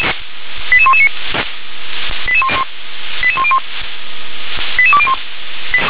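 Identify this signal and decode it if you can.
Five-tone selective call (ZVEI), heard as quick runs of five tones, tones of about 70 ms. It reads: 88119 88101 88201
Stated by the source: ZVEI five-tone selective call received by SDR.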